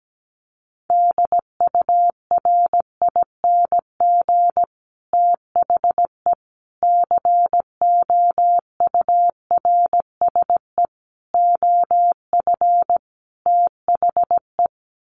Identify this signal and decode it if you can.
Morse code, heard as a keyed tone, about 17 wpm, 700 Hz. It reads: DURING THE COURSE OF THE
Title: Sets of 5 Words Encore 17wpm